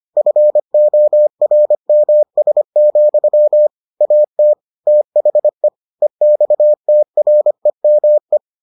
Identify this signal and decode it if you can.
Morse code, heard as a keyed tone, about 25 wpm, 600 Hz. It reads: FORMS, AT THE EXTREME